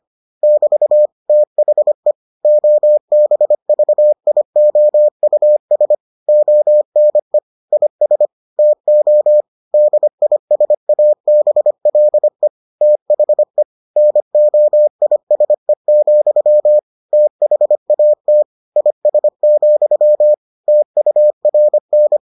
Morse code, heard as a keyed tone, about 25 wpm, 600 Hz. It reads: <BT> THE OBVIOUS ONE IS TO DISABLE THE NOISE, THAT IS, TURN